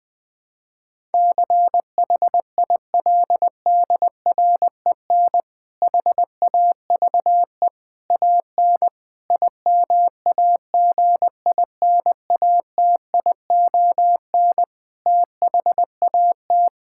Morse code, heard as a keyed tone, about 20 wpm, 700 Hz. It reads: CHILDREN HAVE AN IMAGINATION THAT
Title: Difficult Sentences with Hints - 20wpm